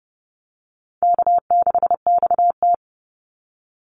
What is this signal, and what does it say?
Morse code, heard as a keyed tone, about 30 wpm, 700 Hz. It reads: K6XT
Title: Easy Call Signs - 30wpm